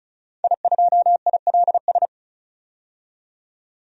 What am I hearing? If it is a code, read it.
Morse code, heard as a keyed tone, about 36 wpm, 700 Hz. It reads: I2ILS
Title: DXCC-100 Call Signs - 5 Characters - 35wpm